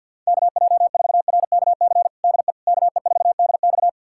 Morse code, heard as a keyed tone, about 47 wpm, 700 Hz. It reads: KJ4RKC DE KE4DX
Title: USA Callsign DE Callsign 50wpm